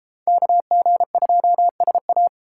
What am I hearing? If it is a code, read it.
Morse code, heard as a keyed tone, about 32 wpm, 700 Hz. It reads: KG2SA